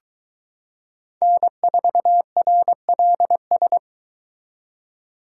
Morse code, heard as a keyed tone, about 22 wpm, 700 Hz. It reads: N4RLS